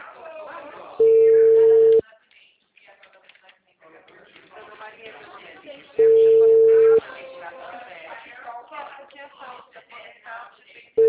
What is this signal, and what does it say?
Ringback tone (China, 450 Hz), with speech.
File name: out- -2007- - - .16649